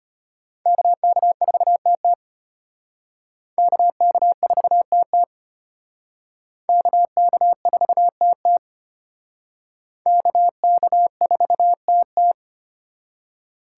Morse code, heard as a keyed tone, about 31 wpm, 700 Hz. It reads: KK4TT KK4TT KK4TT KK4TT
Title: Speed Racing - USA 2x2 Call Signs 25wpm